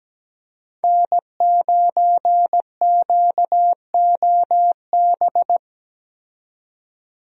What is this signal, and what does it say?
Morse code, heard as a keyed tone, about 17 wpm, 700 Hz. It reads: N9QOB
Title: USA Call Signs - 5 Characters - 17wpm